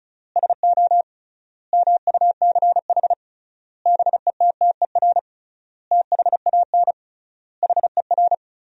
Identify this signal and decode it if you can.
Morse code, heard as a keyed tone, about 36 wpm, 700 Hz. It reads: SO MUCH BETTER THAN HER
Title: 3x Word-Spacing - Sentences from Top 400 Words - 35wpm